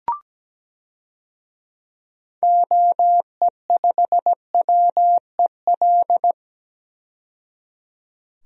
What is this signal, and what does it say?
Morse code, heard as a keyed tone, about 17 wpm, 700 Hz. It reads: OE5WEL
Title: DXCC-100 Call Signs - 6 Characters - 17wpm